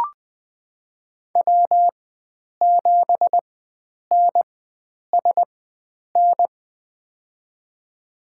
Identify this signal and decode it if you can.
Morse code, heard as a keyed tone, about 21 wpm, spaced as for 9 wpm, 700 Hz. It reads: W7NSN
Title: USA Call Signs - 5 Characters - 4x Character Spacing - 20wpm